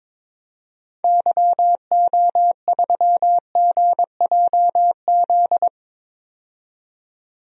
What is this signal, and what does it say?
Morse code, heard as a keyed tone, about 22 wpm, 700 Hz. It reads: YO3GJZ
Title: DXCC-100 Call Signs - 6 Characters - 22wpm